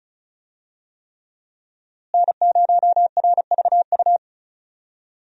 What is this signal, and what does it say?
Morse code, heard as a keyed tone, about 34 wpm, 700 Hz. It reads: N0RVU
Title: USA Call Signs - 5 Characters - Repeated 3x - 35wpm